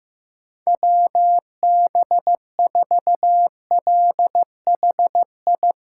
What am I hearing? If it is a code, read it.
Morse code, heard as a keyed tone, about 15 wpm, 700 Hz. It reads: WB4LHI
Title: USA Call Signs - 8x Character Spacing - 15wpm